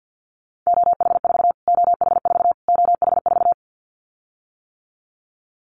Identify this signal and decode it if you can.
Morse code, heard as a keyed tone, about 50 wpm, 700 Hz. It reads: OH4 OH4 OH4
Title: Callsign Prefixes for Top 25 DXCC - Repeated 3x - 50wpm